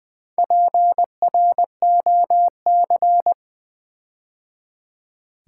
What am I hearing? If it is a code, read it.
Morse code, heard as a keyed tone, about 21 wpm, 700 Hz. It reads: PROC
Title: Build A Word - 4 Letters At A Time - 8-Letter Words - 20wpm